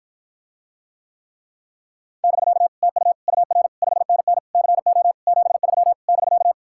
Morse code, heard as a keyed tone, about 54 wpm, 700 Hz. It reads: <BK> TU UR 5NN CO 73 <BK>